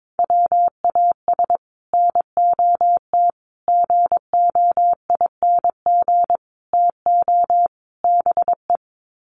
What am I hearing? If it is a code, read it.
Morse code, heard as a keyed tone, about 22 wpm, 700 Hz. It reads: WAS NOT GOING TO BE
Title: Sets of 5 Words Encore 22wpm